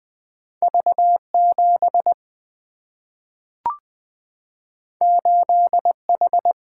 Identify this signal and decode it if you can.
Morse code, heard as a keyed tone, about 20 wpm, 700 Hz. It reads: V 7 8 H
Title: Single-Letter Single-Number 20wpm